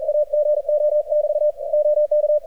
Morse code, heard as a keyed tone, about 39 wpm, 600 Hz. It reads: AGO<BT> OK
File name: archived / CRSnet